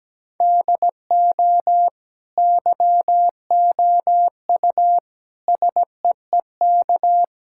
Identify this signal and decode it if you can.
Morse code, heard as a keyed tone, about 17 wpm, 700 Hz. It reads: DO YOU SEEK
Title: Sentences from Top 1600 Words 17wpm